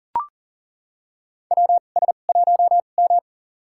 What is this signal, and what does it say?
Morse code, heard as a keyed tone, about 40 wpm, spaced as for 29 wpm, 700 Hz. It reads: WS1M
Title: USA Call Signs - 4 Characters - 2x Character Spacing - 40wpm